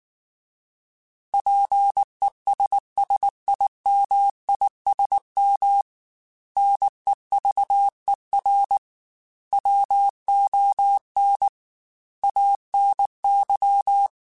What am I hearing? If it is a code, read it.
Morse code, heard as a keyed tone, about 19 wpm, 800 Hz. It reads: PESSIMISM NEVER WON ANY